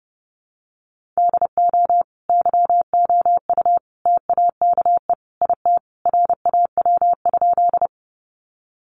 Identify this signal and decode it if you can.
Morse code, heard as a keyed tone, about 30 wpm, 700 Hz. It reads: DO YOU TAKE IT RAW?
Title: Sentences from Top 1500 Words 30wpm